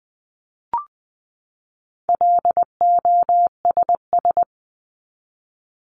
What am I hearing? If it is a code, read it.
Morse code, heard as a keyed tone, about 20 wpm, 700 Hz. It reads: LOSS
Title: Top 1251-1500 Words 20wpm